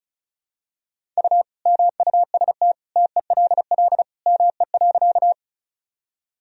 Morse code, heard as a keyed tone, about 36 wpm, 700 Hz. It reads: U MUST TELL ME.
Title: Sentences from Top 200 Words and Abbreviations 35wpm